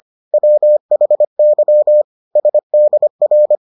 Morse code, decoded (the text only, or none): WHY SDR